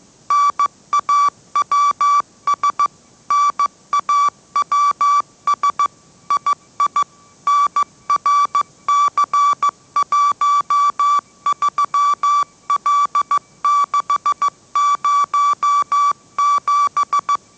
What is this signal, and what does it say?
Morse code, heard as a keyed tone, about 15 wpm, 1160 Hz. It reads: NAWS NAWS II NRC13L607
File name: mrx / mrx065 16wpm